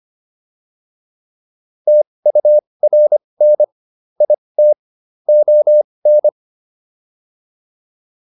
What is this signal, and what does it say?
Morse code, heard as a keyed tone, about 25 wpm, spaced as for 20 wpm, 600 Hz. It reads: TURN IT ON